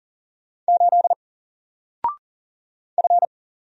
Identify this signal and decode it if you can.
Morse code, heard as a keyed tone, about 40 wpm, 700 Hz. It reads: 8 F
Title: Single Letter-Number 40wpm